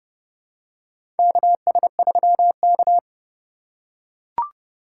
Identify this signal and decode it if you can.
Morse code, heard as a keyed tone, about 32 wpm, 700 Hz. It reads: KS3K
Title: USA Call Signs - 4 Characters - 4x Character Spacing - 30wpm